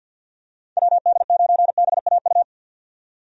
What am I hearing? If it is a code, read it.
Morse code, heard as a keyed tone, about 53 wpm, 700 Hz. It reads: WD9BAU
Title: USA Call Signs - 6 Characters - 50wpm